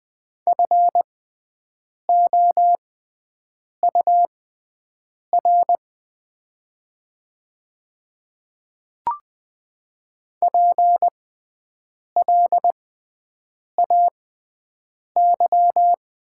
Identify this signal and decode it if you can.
Morse code, heard as a keyed tone, about 20 wpm, spaced as for 7 wpm, 700 Hz. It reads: FOUR PLAY
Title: 6x Character Spacing - Top 300 Words - 20wpm